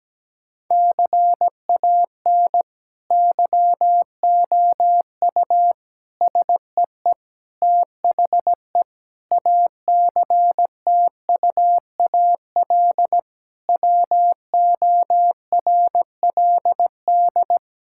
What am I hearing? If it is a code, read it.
Morse code, heard as a keyed tone, about 17 wpm, 700 Hz. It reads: CAN YOU SEE THE ACTUAL WORLD
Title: Sentences from Top 1000 Words - Review Part 5 - 17wpm